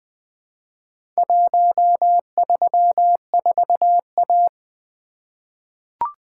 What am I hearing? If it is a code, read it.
Morse code, heard as a keyed tone, about 20 wpm, 700 Hz. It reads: 134A
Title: Four Letter-Number Combinations - 20wpm